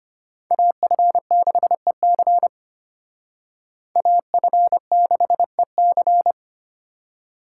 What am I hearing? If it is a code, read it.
Morse code, heard as a keyed tone, about 32 wpm, 700 Hz. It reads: AF6EC AF6EC